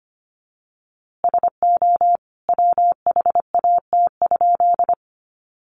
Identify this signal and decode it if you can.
Morse code, heard as a keyed tone, about 25 wpm, 700 Hz. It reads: SO WHAT?
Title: A Sound Of Thunder 25wpm